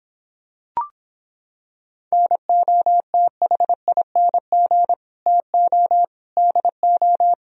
Morse code, heard as a keyed tone, about 26 wpm, 700 Hz. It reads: NOTHING TO DO